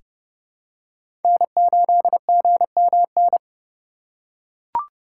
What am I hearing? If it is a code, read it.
Morse code, heard as a keyed tone, about 32 wpm, 700 Hz. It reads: N8GMN
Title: USA Call Signs - 5 Characters - 2x Character Spacing - 30wpm